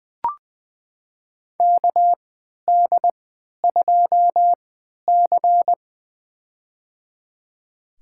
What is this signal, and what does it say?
Morse code, heard as a keyed tone, about 20 wpm, spaced as for 11 wpm, 700 Hz. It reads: KD2C
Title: USA Call Signs - 4 Characters - 3x Character Spacing - 20wpm